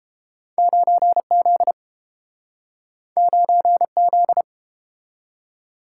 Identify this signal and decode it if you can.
Morse code, heard as a keyed tone, about 31 wpm, 700 Hz. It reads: 9Z 9Z